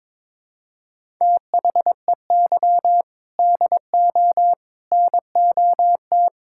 Morse code, heard as a keyed tone, about 22 wpm, 700 Hz. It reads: THEY DO NOT
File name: practice / 22wpm / Sentences with 3-Word Phrases - 22wpm